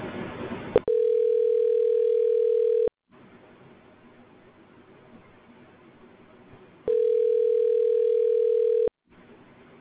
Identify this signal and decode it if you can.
Ringback tone (North America), 440 + 480 Hz.
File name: recordings / RE7cd526fea83d55d0d4b53a67d2a3e7f0